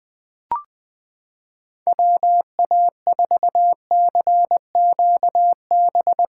Morse code, heard as a keyed tone, about 20 wpm, 700 Hz. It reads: WA4CQB
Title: USA Callsign DE Callsign - 6 Characters - 20wpm